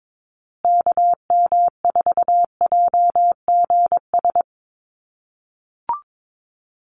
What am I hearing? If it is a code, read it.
Morse code, heard as a keyed tone, about 22 wpm, 700 Hz. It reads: KM4JGS
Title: USA Call Signs Repeated 3x 22wpm